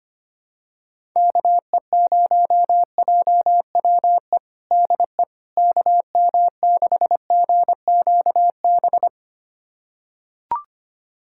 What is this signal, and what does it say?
Morse code, heard as a keyed tone, about 25 wpm, 700 Hz. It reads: KE0JWE DE KM6GQB
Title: USA Callsign DE Callsign 25wpm